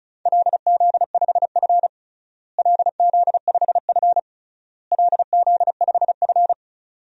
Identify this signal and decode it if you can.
Morse code, heard as a keyed tone, about 36 wpm, 700 Hz. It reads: LZ5F LZ5F LZ5F
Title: DXCC-100 Call Signs - 4 Characters - Repeated 3x - 35wpm